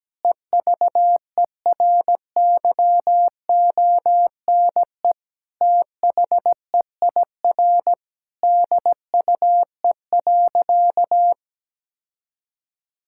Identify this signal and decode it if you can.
Morse code, heard as a keyed tone, about 17 wpm, 700 Hz. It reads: EVERYONE THEIR DUE.
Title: Sentences from Top 1000 Words - Review Part 6 - 17wpm